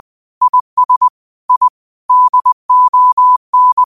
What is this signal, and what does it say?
Morse code, heard as a keyed tone, about 20 wpm, 1000 Hz. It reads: IS I DON